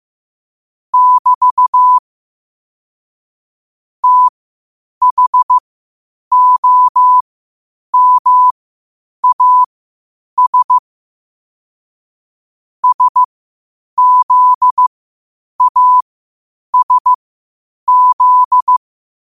Morse code, heard as a keyed tone, about 15 wpm, spaced as for 8 wpm, 1000 Hz. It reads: <BT> THOMAS SZASZ
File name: 8wpm